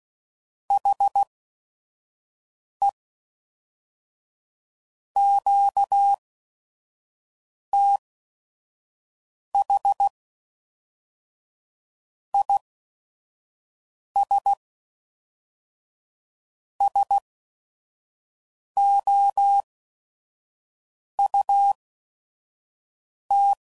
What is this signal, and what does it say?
Morse code, heard as a keyed tone, about 16 wpm, spaced as for 5 wpm, 800 Hz. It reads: HE QTH IS SOUT
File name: CW Code Groups @5WPM / QSOs / kc7sac